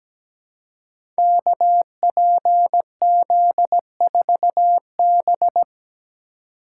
Morse code, heard as a keyed tone, about 17 wpm, 700 Hz. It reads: KPZ4B
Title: Code Groups - 5 Letter:Numbers - 17wpm